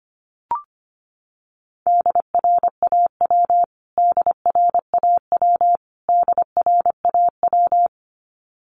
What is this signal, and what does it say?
Morse code, heard as a keyed tone, about 25 wpm, 700 Hz. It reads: DRAW DRAW DRAW
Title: Top 500 Words Repeated 25wpm